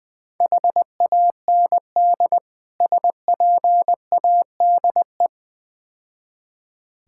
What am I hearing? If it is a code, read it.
Morse code, heard as a keyed tone, about 20 wpm, 700 Hz. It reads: HAND SPADE
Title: Difficult Words - Speed-Racing with Hints - 20wpm